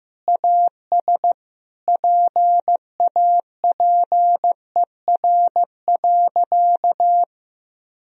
Morse code, heard as a keyed tone, about 16 wpm, 700 Hz. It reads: AS PAPER.
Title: Sentences from Top 1000 Words - Review Part 1 - 15wpm